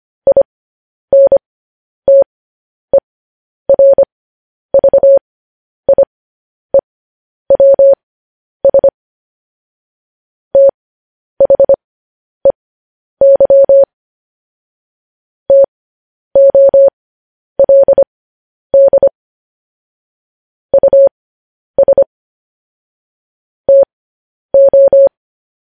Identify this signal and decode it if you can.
Morse code, heard as a keyed tone, about 25 wpm, spaced as for 10 wpm, 550 Hz. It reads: INTERVIEWS THEY TOLD US TO
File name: podcast / News headlines-1x-25x10